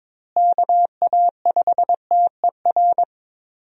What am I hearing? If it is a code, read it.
Morse code, heard as a keyed tone, about 22 wpm, 700 Hz. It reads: KA5TER
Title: USA Call Signs - 6 Characters - Repeated 3x - 22wpm